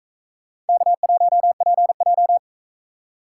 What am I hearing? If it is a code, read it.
Morse code, heard as a keyed tone, about 44 wpm, 700 Hz. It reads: K1PJ